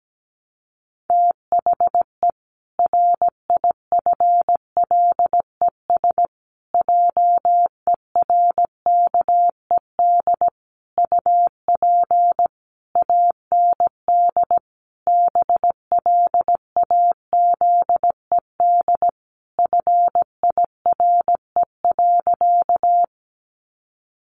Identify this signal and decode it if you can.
Morse code, heard as a keyed tone, about 17 wpm, 700 Hz. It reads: THE RIFLES JERKED UP AND BLAZED FIRE.
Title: A Sound Of Thunder 17wpm